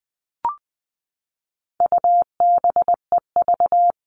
Morse code, heard as a keyed tone, about 20 wpm, 700 Hz. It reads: UBEV
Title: All Cut Numbers - 4-Digit 20wpm